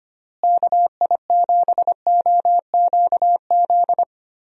Morse code, heard as a keyed tone, about 25 wpm, 700 Hz. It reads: KI7OQZ
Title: USA Call Signs - 6 Characters - Repeated 3x - 25wpm